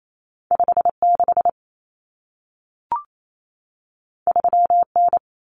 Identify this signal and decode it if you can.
Morse code, heard as a keyed tone, about 28 wpm, 700 Hz. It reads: 56 3N
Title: Common Cut Numbers - 2-Digit 28wpm